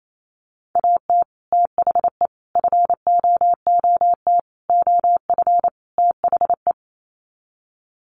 Morse code, heard as a keyed tone, about 28 wpm, 700 Hz. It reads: AT THE FOOT OF THE
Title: Sets of 5 Words 28wpm